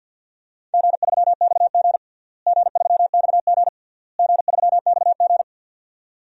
Morse code, heard as a keyed tone, about 52 wpm, 700 Hz. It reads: G3XG G3XG G3XG
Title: DXCC-100 Call Signs - 4 Characters - Repeated 3x - 50wpm